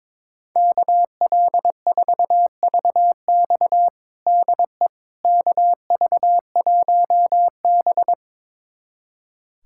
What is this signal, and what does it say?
Morse code, heard as a keyed tone, about 22 wpm, 700 Hz. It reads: KL4VX DE KV1B
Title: USA Callsign DE Callsign 22wpm